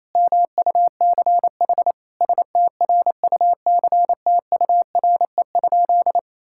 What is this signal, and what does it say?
Morse code, heard as a keyed tone, about 28 wpm, 700 Hz. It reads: MUCH STRUCTURE?
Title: Sentences from Top 1000 Words - Review Part 6 - 28wpm